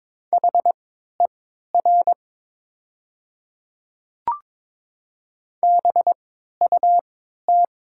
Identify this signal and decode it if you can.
Morse code, heard as a keyed tone, about 23 wpm, spaced as for 13 wpm, 700 Hz. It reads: HER BUT
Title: 3x Character Spacing - Top 100 Words - 22wpm